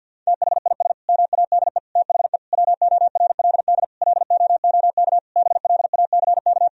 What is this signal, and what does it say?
Morse code, heard as a keyed tone, about 50 wpm, 700 Hz. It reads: THIS MADE THE WORLD LOOK BLACK
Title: Sentences from Top 1000 Words - Review Part 2 - 50wpm